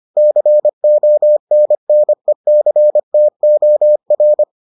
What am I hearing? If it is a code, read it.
Morse code, heard as a keyed tone, about 25 wpm, 600 Hz. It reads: CONNECTOR